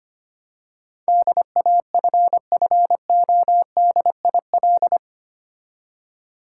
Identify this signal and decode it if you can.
Morse code, heard as a keyed tone, about 25 wpm, 700 Hz. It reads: DAFFODIL
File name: practice / 25wpm / Difficult Words with Hints - 25wpm